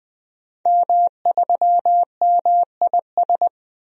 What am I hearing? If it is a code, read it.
Morse code, heard as a keyed tone, about 20 wpm, 700 Hz. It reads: M3MIS
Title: DXCC-100 Call Signs - 5 Characters - 20wpm